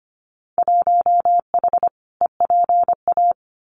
Morse code, heard as a keyed tone, about 25 wpm, 690 Hz. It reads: 1H EPA
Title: Winter Field Day Exchanges 25wpm